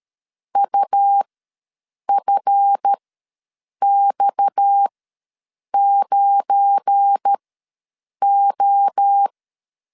Morse code, heard as a keyed tone, about 13 wpm, spaced as for 7 wpm, 800 Hz. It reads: UFX9O
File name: LEZIONI / 8 YJ45 / 8 YJ45 ws 12wpm 3146